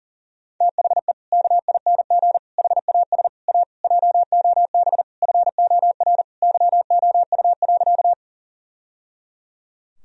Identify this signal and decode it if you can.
Morse code, heard as a keyed tone, about 37 wpm, 700 Hz. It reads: THE KING HAS A JOB FOR YOU.
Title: Sentences from Top 1000 Words - Review Part 3 - 40wpm